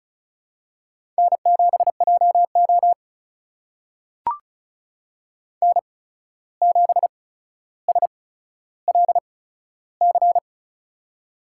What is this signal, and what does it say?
Morse code, heard as a keyed tone, about 36 wpm, spaced as for 14 wpm, 700 Hz. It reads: N7JO N7SLC
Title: USA Call Signs - 8x Character Spacing - 35wpm